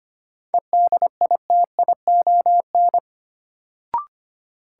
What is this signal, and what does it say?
Morse code, heard as a keyed tone, about 25 wpm, 700 Hz. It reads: EDITION
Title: Sets of 5 Words - New Word Drill - 25wpm